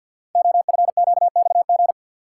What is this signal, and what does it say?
Morse code, heard as a keyed tone, about 48 wpm, 700 Hz. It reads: OFQXG
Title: Code Groups - 5 Letters - 50wpm